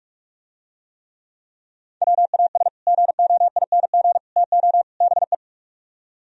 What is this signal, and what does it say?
Morse code, heard as a keyed tone, about 49 wpm, 700 Hz. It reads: WAS GOING TO BE